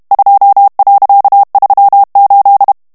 Morse code, heard as a keyed tone, about 32 wpm, 800 Hz. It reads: 2.38
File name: HOLLANDAIS texte clair 1 vitesse 30 mots minute